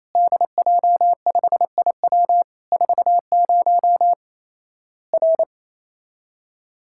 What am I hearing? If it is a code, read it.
Morse code, heard as a keyed tone, about 28 wpm, 700 Hz. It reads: DJ5IW 40 R
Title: Worked All Europe DX Contest - QTC Exchange - 28wpm